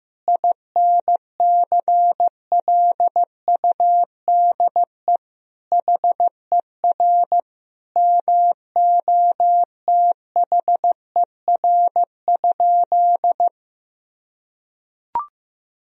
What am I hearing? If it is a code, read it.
Morse code, heard as a keyed tone, about 15 wpm, 700 Hz. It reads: INCLUDE HER MOTHER?
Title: Sentences from Top 1000 Words - Review Part 3 - 15wpm